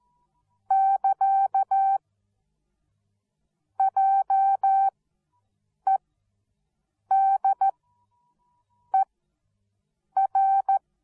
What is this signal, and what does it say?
Morse code, heard as a keyed tone, about 14 wpm, spaced as for 6 wpm, 780 Hz. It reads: <KA> JEDER
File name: Onlinelehrgang / morsen / DJ4UF Morsen13